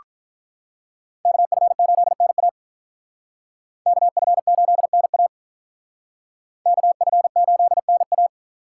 Morse code, heard as a keyed tone, about 41 wpm, 700 Hz. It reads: KF8NA KF8NA KF8NA
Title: Speed Racing - USA 2x2 Call Signs 35wpm